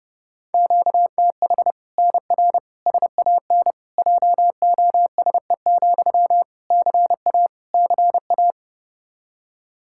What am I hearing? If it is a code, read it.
Morse code, heard as a keyed tone, about 32 wpm, 700 Hz. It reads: QTH NR SAN JOSE, CA CA
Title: Practice QSO - Exchange 1 - Beginning - 30wpm